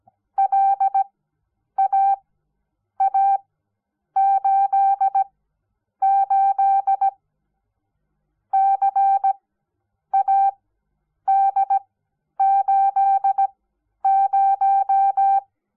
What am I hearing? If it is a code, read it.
Morse code, heard as a keyed tone, about 17 wpm, spaced as for 9 wpm, 780 Hz. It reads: LAA88 CAD80
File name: Onlinelehrgang / morsen / DJ4UF Morsen06